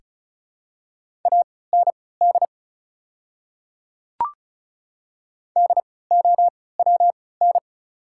Morse code, heard as a keyed tone, about 36 wpm, spaced as for 20 wpm, 700 Hz. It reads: AND DOWN
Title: 3x Character Spacing - Top 200 Words - 35wpm